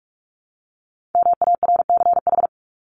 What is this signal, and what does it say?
Morse code, heard as a keyed tone, about 44 wpm, 700 Hz. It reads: MARCH
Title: Top 1501-1750 Words 45wpm